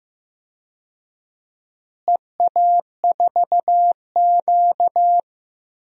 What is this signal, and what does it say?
Morse code, heard as a keyed tone, about 15 wpm, 700 Hz. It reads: EA4Q
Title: DXCC-100 Call Signs - 4 Characters - Repeated 2x - 15wpm